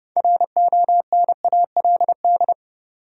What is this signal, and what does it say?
Morse code, heard as a keyed tone, about 30 wpm, 700 Hz. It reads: RONALD
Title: USA Common Names - 30wpm